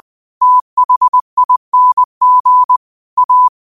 Morse code, heard as a keyed tone, about 20 wpm, 1000 Hz. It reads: THING A
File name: qotdfiles / 20wpm